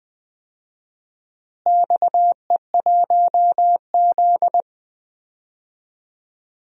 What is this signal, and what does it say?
Morse code, heard as a keyed tone, about 20 wpm, 700 Hz. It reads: XE1Z
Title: DXCC-100 Call Signs - 4 Characters - Repeated 3x - 20wpm